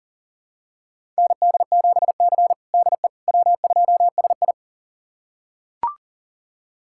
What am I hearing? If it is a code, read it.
Morse code, heard as a keyed tone, about 37 wpm, 700 Hz. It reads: ND7C DE W2SI
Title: USA Callsign DE Callsign - 4 Characters - 40wpm